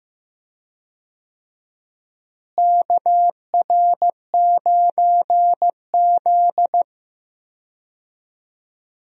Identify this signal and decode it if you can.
Morse code, heard as a keyed tone, about 15 wpm, 700 Hz. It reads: KR9Z
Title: USA Call Signs - 4 Characters - Repeated 3x - 15wpm